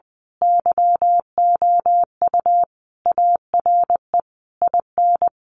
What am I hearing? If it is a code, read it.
Morse code, heard as a keyed tone, about 20 wpm, 700 Hz. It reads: YOU ARE IN
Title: Sentences from Top 1500 Words 20wpm